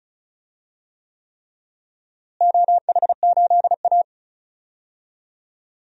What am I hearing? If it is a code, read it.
Morse code, heard as a keyed tone, about 34 wpm, 700 Hz. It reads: OH8A
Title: DXCC-100 Call Signs - 4 Characters - Repeated 3x - 35wpm